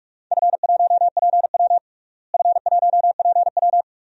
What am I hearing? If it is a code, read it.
Morse code, heard as a keyed tone, about 43 wpm, 700 Hz. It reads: F1PW F1PW
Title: DXCC-100 Call Signs - 4 Characters - Repeated 2x - 45wpm